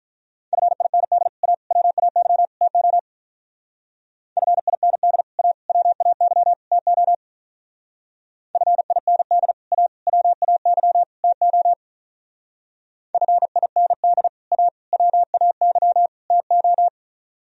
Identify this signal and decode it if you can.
Morse code, heard as a keyed tone, about 42 wpm, 700 Hz. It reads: FIND A WAY TO FIND A WAY TO FIND A WAY TO FIND A WAY TO
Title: Speed-Racing - Sets of 4 Words 35wpm